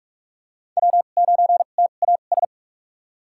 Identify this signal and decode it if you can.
Morse code, heard as a keyed tone, about 44 wpm, spaced as for 32 wpm, 700 Hz. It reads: W9TAS
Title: USA Call Signs - 5 Characters - 2x Character Spacing - 45wpm